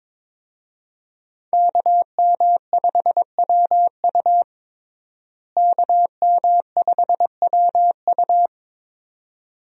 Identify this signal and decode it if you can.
Morse code, heard as a keyed tone, about 22 wpm, 700 Hz. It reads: KM5WU KM5WU
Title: USA Call Signs - 5 Characters - Repeated 2x - 22wpm